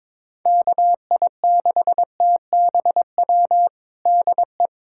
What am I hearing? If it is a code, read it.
Morse code, heard as a keyed tone, about 22 wpm, 700 Hz. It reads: KI6TBW DE
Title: USA Callsign DE Callsign - 6 Characters - 22wpm